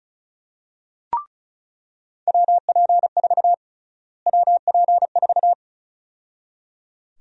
Morse code, heard as a keyed tone, about 34 wpm, 700 Hz. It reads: WP4 WP4
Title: USA Call Sign Prefixes - Repeated 2x - 35wpm